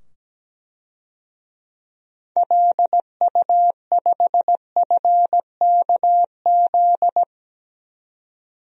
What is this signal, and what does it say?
Morse code, heard as a keyed tone, about 17 wpm, 700 Hz. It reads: LU5FKZ
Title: DXCC-100 Call Signs - 6 Characters - 17wpm